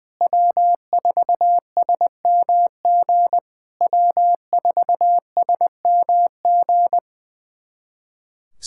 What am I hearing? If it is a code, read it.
Morse code, heard as a keyed tone, about 20 wpm, 700 Hz. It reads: W4SMG W4SMG
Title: SOTA QSO Practice 20wpm